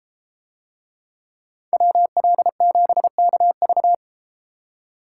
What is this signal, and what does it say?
Morse code, heard as a keyed tone, about 33 wpm, 700 Hz. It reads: WL7KV